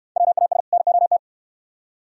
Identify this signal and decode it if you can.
Morse code, heard as a keyed tone, about 69 wpm, 700 Hz. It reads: WAS NOT